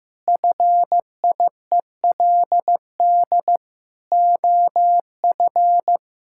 Morse code, heard as a keyed tone, about 15 wpm, 700 Hz. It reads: FIELD OF
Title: Pangram Challenge v2 - 15wpm